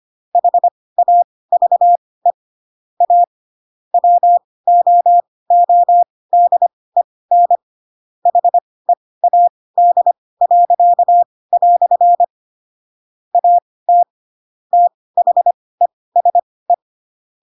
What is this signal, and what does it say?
Morse code, heard as a keyed tone, about 25 wpm, spaced as for 18 wpm, 700 Hz. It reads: HAVE A WOODEN HEAD." AT THESE
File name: Pinocchio / Pinocchio ch4 12 25 ik1hge